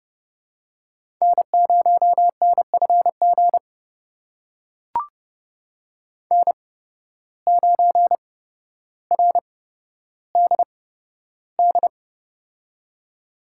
Morse code, heard as a keyed tone, about 30 wpm, spaced as for 13 wpm, 700 Hz. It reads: N0NFG N9RDD
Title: USA Call Signs - 5 Characters - 8x Character Spacing - 30wpm